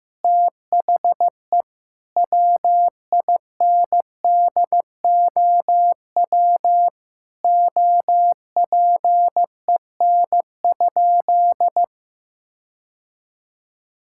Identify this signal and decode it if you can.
Morse code, heard as a keyed tone, about 15 wpm, 700 Hz. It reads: THE WINDOW OPEN?
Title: Sentences from Top 1600 Words 15wpm